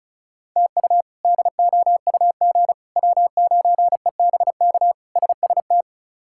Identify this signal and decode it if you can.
Morse code, heard as a keyed tone, about 34 wpm, 700 Hz. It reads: TU DOUG W9EBK SST
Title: K1USN SST QSO Practice - 35wpm